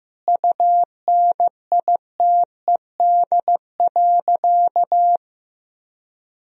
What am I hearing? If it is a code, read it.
Morse code, heard as a keyed tone, about 15 wpm, 700 Hz. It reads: UNITED.
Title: Sentences from Top 1000 Words - Review Part 4 - 15wpm